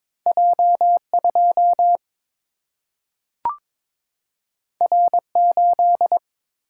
Morse code, heard as a keyed tone, about 22 wpm, 700 Hz. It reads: J2 R8
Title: Single-Letter Single-Number 22wpm